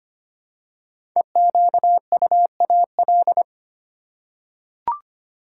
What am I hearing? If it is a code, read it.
Morse code, heard as a keyed tone, about 25 wpm, 700 Hz. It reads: EQUAL
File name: 5 Letter Words - First 3 Letters Given - 25wpm